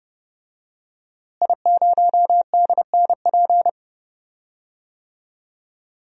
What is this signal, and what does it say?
Morse code, heard as a keyed tone, about 30 wpm, 700 Hz. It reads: I0DNP
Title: DXCC-100 Call Signs - 5 Characters - Repeated 3x - 30wpm